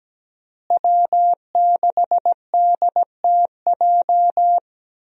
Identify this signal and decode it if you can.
Morse code, heard as a keyed tone, about 17 wpm, 700 Hz. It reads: W6DTJ
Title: USA Call Signs - 5 Characters - Repeated 2x - 17wpm